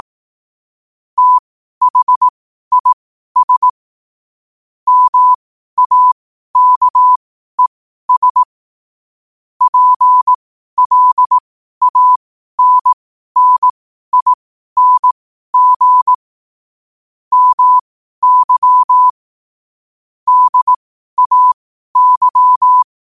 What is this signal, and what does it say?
Morse code, heard as a keyed tone, about 18 wpm, spaced as for 12 wpm, 1000 Hz. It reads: THIS MAKES PLANNING MY DAY
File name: qotdfiles / 12wpm